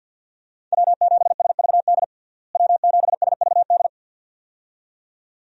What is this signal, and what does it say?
Morse code, heard as a keyed tone, about 52 wpm, 700 Hz. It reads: W7SVD W7SVD
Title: USA Call Signs - 5 Characters - Repeated 2x - 50wpm